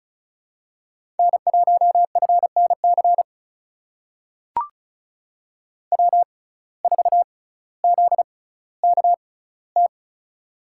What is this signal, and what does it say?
Morse code, heard as a keyed tone, about 36 wpm, 700 Hz. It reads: N 1 F N C W 4 Z K T
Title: USA Call Signs - 5 Characters - 6x Character Spacing - 35wpm